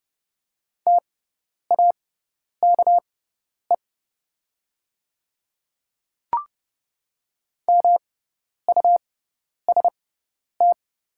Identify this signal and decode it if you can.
Morse code, heard as a keyed tone, about 30 wpm, spaced as for 10 wpm, 690 Hz. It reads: TAKE MUST
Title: 6x Character Spacing - Top 200 Words - 30wpm